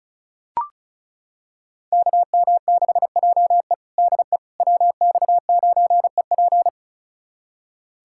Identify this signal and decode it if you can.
Morse code, heard as a keyed tone, about 36 wpm, 700 Hz. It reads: KM6JE DE WX9EP
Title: USA Callsign DE Callsign 35wpm